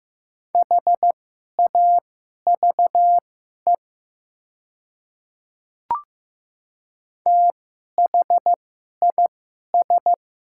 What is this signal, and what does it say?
Morse code, heard as a keyed tone, about 15 wpm, spaced as for 11 wpm, 700 Hz. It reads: HAVE THIS
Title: 2x Character Spacing - Top 200 Words - 15wpm